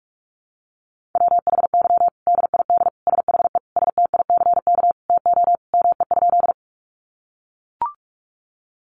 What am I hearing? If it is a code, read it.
Morse code, heard as a keyed tone, about 44 wpm, 700 Hz. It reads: WHY DID SHE STICK TO ME?
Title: Sentences from Top 1200 Words 45wpm